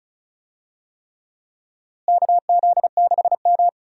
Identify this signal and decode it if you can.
Morse code, heard as a keyed tone, about 34 wpm, 700 Hz. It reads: KZ6M
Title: USA Call Signs - 4 Characters - Repeated 3x - 35wpm